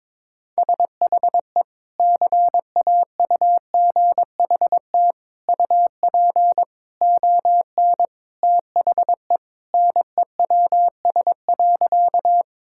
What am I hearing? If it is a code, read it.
Morse code, heard as a keyed tone, about 22 wpm, 700 Hz. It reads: SHE CAUGHT UP ON THE NEWS.
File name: practice / 22wpm / Sentences with 3-Word Phrases - 22wpm